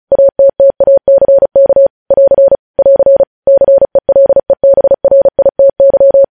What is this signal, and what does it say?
Morse code, heard as a keyed tone, about 36 wpm, 550 Hz. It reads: ATTACK <AR> <AR> CELEBRITY
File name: podcast / News headlines-1x-35x35